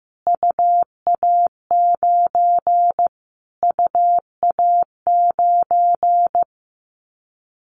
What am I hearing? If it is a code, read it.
Morse code, heard as a keyed tone, about 15 wpm, 700 Hz. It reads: UA9 UA9
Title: Callsign Prefixes for Top 25 DXCC - Repeated 3x - 15wpm